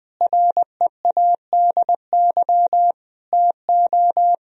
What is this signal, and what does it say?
Morse code, heard as a keyed tone, about 20 wpm, 700 Hz. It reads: READY TO
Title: Sentences from Top 1000 Words - Review Part 1 - 20wpm